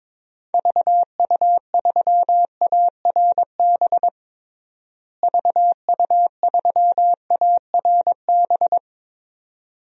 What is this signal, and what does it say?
Morse code, heard as a keyed tone, about 22 wpm, 700 Hz. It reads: VU3ARB VU3ARB
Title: DXCC-100 Call Signs - 6 Characters - Repeated 2x - 22wpm